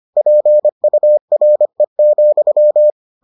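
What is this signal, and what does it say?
Morse code, heard as a keyed tone, about 25 wpm, 600 Hz. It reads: PURE,
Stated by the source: Morse edition of the Foundations of Amateur Radio episodes.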